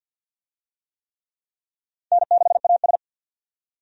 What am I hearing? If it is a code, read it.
Morse code, heard as a keyed tone, about 52 wpm, 700 Hz. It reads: N6AS
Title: USA Call Signs - 4 Characters - 50wpm